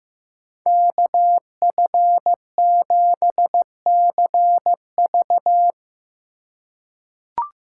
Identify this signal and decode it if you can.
Morse code, heard as a keyed tone, about 15 wpm, 700 Hz. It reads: KF7CV
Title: USA Call Signs - 3x Character Spacing - 15wpm